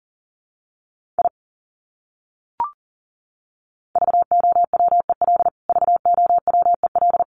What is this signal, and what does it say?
Morse code, heard as a keyed tone, about 38 wpm, 700 Hz. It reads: I VOWEL VOWEL
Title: Top 500 Words Repeated 40wpm